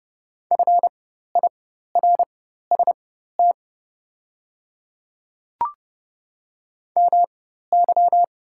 Morse code, heard as a keyed tone, about 30 wpm, spaced as for 14 wpm, 700 Hz. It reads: FIRST MY
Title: 4x Character Spacing - Top 300 Words - 30wpm